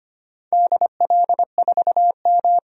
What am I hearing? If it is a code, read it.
Morse code, heard as a keyed tone, about 25 wpm, 700 Hz. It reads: DL4M
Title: DXCC-100 Call Signs - 4 Characters - 25wpm